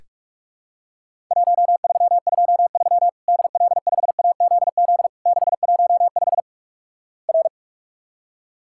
Morse code, heard as a keyed tone, about 46 wpm, 700 Hz. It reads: 1323 DL5AZZ 615 R
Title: Worked All Europe DX Contest - QTC Exchange - 45wpm